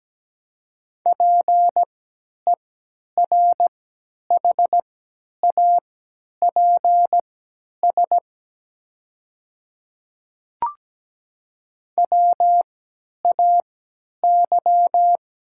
Morse code, heard as a keyed tone, about 17 wpm, spaced as for 10 wpm, 690 Hz. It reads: PERHAPS WAY
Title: 3x Character Spacing - Top 500 Words - 17wpm